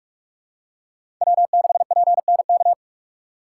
Morse code, heard as a keyed tone, about 46 wpm, 690 Hz. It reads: W6PNK
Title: USA Call Signs - 5 Characters - 6x Character Spacing - 45wpm